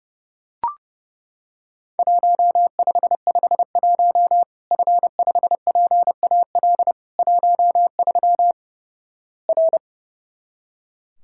Morse code, heard as a keyed tone, about 32 wpm, 700 Hz. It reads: 1551 F5PAL 13 R
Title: Worked All Europe DX Contest - QTC Exchange - 30wpm